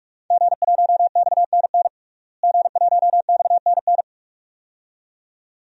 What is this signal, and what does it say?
Morse code, heard as a keyed tone, about 46 wpm, 700 Hz. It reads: G1XNN G1XNN
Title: DXCC-100 Call Signs - 5 Characters - Repeated 2x - 45wpm